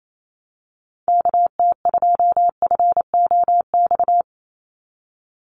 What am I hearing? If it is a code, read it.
Morse code, heard as a keyed tone, about 28 wpm, 700 Hz. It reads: KT2FOX
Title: USA Call Signs 28wpm